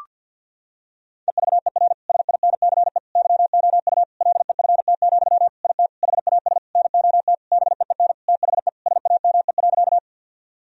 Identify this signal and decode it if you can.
Morse code, heard as a keyed tone, about 51 wpm, 700 Hz. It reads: EVER SINCE YOU LEFT, IT HAS NOT BEEN THE SAME.
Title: Sentences from Top 1000 Words - Review Part 1 - 50wpm